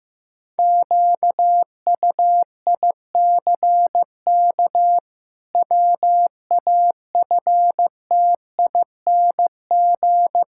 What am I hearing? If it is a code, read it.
Morse code, heard as a keyed tone, about 16 wpm, 700 Hz. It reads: QUICK WAFTING
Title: Pangram Challenge v2 - 15wpm